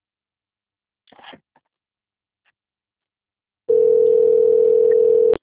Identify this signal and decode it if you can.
Ringback tone (North America), 440 + 480 Hz.